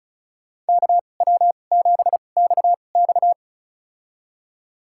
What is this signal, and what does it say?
Morse code, heard as a keyed tone, about 36 wpm, spaced as for 25 wpm, 700 Hz. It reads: KW7XX
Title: USA Call Signs - 5 Characters - 2x Character Spacing - 35wpm